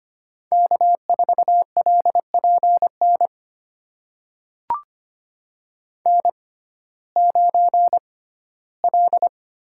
Morse code, heard as a keyed tone, about 25 wpm, spaced as for 15 wpm, 700 Hz. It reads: K4LPN N9L
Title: USA Call Signs - 5 Characters - 6x Character Spacing - 25wpm